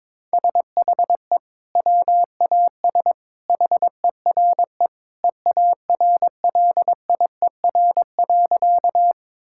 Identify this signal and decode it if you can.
Morse code, heard as a keyed tone, about 22 wpm, 700 Hz. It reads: SHE WAS HERE EARLIER.
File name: practice / 22wpm / Sentences with 2-Word Phrases - 22wpm